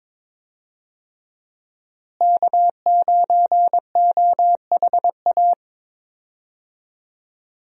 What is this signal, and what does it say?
Morse code, heard as a keyed tone, about 22 wpm, 700 Hz. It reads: K9OHA
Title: USA Call Signs - 5 Characters - Repeated 3x - 22wpm